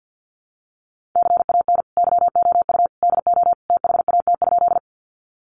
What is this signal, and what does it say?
Morse code, heard as a keyed tone, about 50 wpm, 700 Hz. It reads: CAN YOU DO THAT?
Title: Sentences from Top 100 Words 50wpm